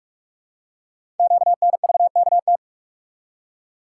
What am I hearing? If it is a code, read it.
Morse code, heard as a keyed tone, about 44 wpm, 700 Hz. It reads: QNVKT
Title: Code Groups - 5 Letter:Numbers - 45wpm